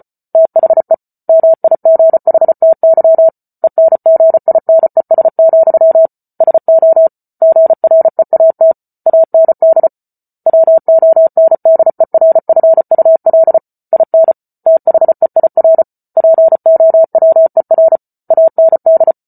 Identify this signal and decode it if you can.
Morse code, heard as a keyed tone, about 34 wpm, 650 Hz. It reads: THE MIGHTY ENGINES, SO GREAT AND WONDERFUL IN THEIR POWER AND